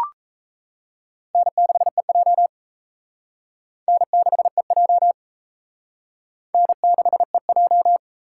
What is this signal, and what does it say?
Morse code, heard as a keyed tone, about 35 wpm, 700 Hz. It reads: N6EJ N6EJ N6EJ
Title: Speed Racing - USA Call Signs 28wpm